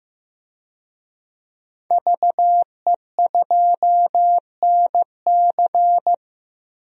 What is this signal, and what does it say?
Morse code, heard as a keyed tone, about 15 wpm, 700 Hz. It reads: VE2NC
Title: DXCC-100 Call Signs - 5 Characters - Repeated 2x - 15wpm